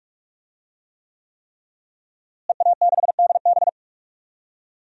Morse code, heard as a keyed tone, about 43 wpm, 700 Hz. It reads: EA6DB
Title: DXCC-100 Call Signs - 5 Characters - Repeated 3x - 45wpm